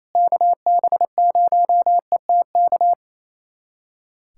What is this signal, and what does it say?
Morse code, heard as a keyed tone, about 28 wpm, 700 Hz. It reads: KB0ETK
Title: USA Callsign DE Callsign 28wpm